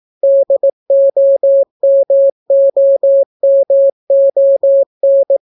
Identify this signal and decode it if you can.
Morse code, heard as a keyed tone, about 18 wpm, 550 Hz. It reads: DOMOMON